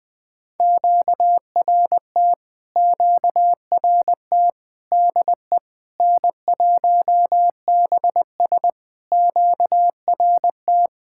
Morse code, heard as a keyed tone, about 20 wpm, 700 Hz. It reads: QRT QRT DE N1BS QRT
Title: POTA QSO Practice - 20wpm